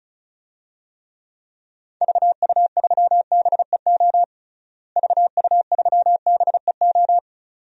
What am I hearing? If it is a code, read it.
Morse code, heard as a keyed tone, about 34 wpm, 700 Hz. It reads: VU3BEO VU3BEO
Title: DXCC-100 Call Signs - 6 Characters - Repeated 2x - 35wpm